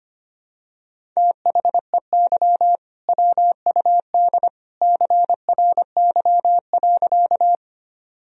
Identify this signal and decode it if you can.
Morse code, heard as a keyed tone, about 25 wpm, 700 Hz. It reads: THEY WUD CRY.
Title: Sentences from Top 500 Words and Abbreviations 25wpm